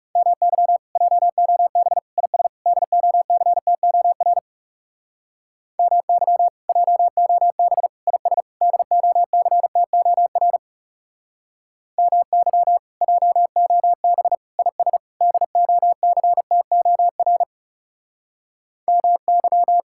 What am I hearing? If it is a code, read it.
Morse code, heard as a keyed tone, about 35 wpm, 700 Hz. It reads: MY JOB IS DOCTOR MY JOB IS DOCTOR MY JOB IS DOCTOR MY
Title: Speed-Racing Common QSO Phrases Encore - 30wpm